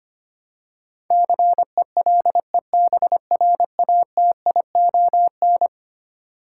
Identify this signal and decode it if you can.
Morse code, heard as a keyed tone, about 25 wpm, 700 Hz. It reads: CELEBRATION
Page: morse-code-ninja-practice-25wpm